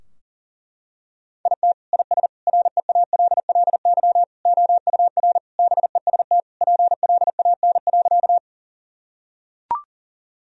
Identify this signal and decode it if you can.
Morse code, heard as a keyed tone, about 40 wpm, 700 Hz. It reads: IT IS REALLY OUR BEST PLAN.
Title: Sentences from Top 1000 Words - Review Part 2 - 40wpm